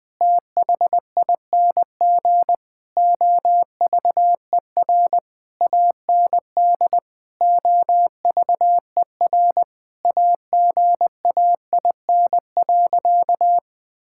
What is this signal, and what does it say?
Morse code, heard as a keyed tone, about 20 wpm, 700 Hz. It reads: THING OVER AND OVER AGAIN.
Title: Sentences with 6-Word Phrases - Spoken Beginning - 20wpm